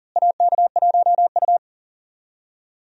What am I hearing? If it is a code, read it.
Morse code, heard as a keyed tone, about 40 wpm, 700 Hz. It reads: AK1U
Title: USA Call Signs - 4 Characters - Repeated 2x - 40wpm